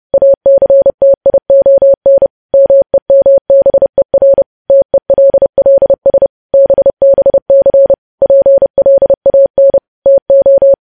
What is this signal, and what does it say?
Morse code, heard as a keyed tone, about 30 wpm, 550 Hz. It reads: ACTION MEMBER TELLS BBC PLAN TO
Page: news headlines in morse code at 30 wpm